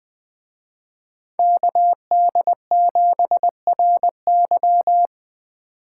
Morse code, heard as a keyed tone, about 20 wpm, 700 Hz. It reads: KD7RY
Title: Speed Racing - USA 2x2 Call Signs 17wpm